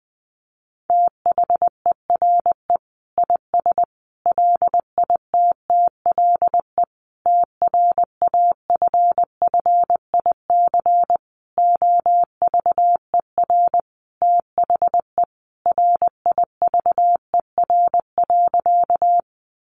Morse code, heard as a keyed tone, about 20 wpm, 700 Hz. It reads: THERE IS LITTLE TRAFFIC OVER THE RIVER.
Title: Sentences from Top 1300 Words 20wpm